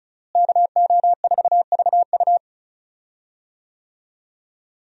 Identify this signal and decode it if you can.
Morse code, heard as a keyed tone, about 34 wpm, 700 Hz. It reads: KO4VU
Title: USA Call Signs - 5 Characters - Repeated 2x - 35wpm